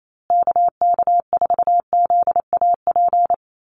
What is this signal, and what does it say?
Morse code, heard as a keyed tone, about 28 wpm, 700 Hz. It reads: KK4ZAP
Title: USA Call Signs Repeated 2x 28wpm